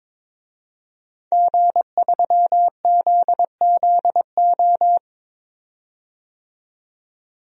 Morse code, heard as a keyed tone, about 22 wpm, 700 Hz. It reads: G3ZZO
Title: DXCC-100 Call Signs - 5 Characters - 22wpm